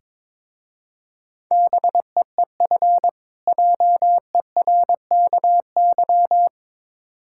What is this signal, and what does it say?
Morse code, heard as a keyed tone, about 22 wpm, 690 Hz. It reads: BEEF JERKY
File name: Difficult Words with Hints - 22wpm